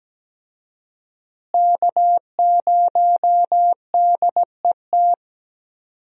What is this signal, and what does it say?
Morse code, heard as a keyed tone, about 17 wpm, 700 Hz. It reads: K0DET
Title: USA Call Signs - 5 Characters - Repeated 3x - 17wpm